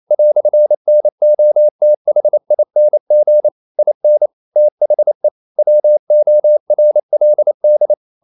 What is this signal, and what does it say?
Morse code, heard as a keyed tone, about 28 wpm, 600 Hz. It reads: "NOTHING IN THE WORLD
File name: CW / Practice / EnglishQuotations0128